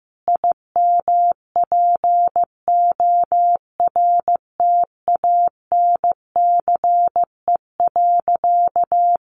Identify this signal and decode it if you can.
Morse code, heard as a keyed tone, about 15 wpm, 700 Hz. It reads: IMPORTANCE.
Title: Sentences from Top 1200 Words 15wpm